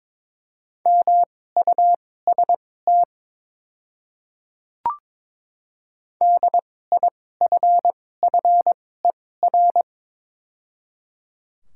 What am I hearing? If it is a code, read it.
Morse code, heard as a keyed tone, about 23 wpm, spaced as for 16 wpm, 700 Hz. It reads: MUST DIFFER
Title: 2x Character Spacing - Top 200 Words - 22wpm